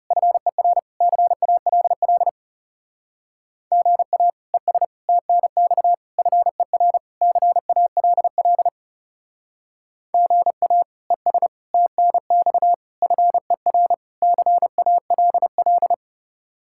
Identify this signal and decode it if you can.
Morse code, heard as a keyed tone, about 36 wpm, 700 Hz. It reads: FER CALL GA ES TNX FER CALL GA ES TNX FER CALL
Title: Speed-Racing Common QSO Phrases Encore - 30wpm